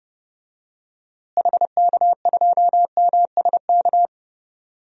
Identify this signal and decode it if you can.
Morse code, heard as a keyed tone, about 30 wpm, 700 Hz. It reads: HK2MSK
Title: DXCC-100 Call Signs - 6 Characters - Repeated 3x - 30wpm